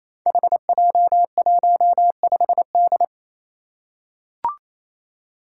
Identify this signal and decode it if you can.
Morse code, heard as a keyed tone, about 28 wpm, 700 Hz. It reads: HJ15D
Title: Five Challenging Letter-Number Combinations 28wpm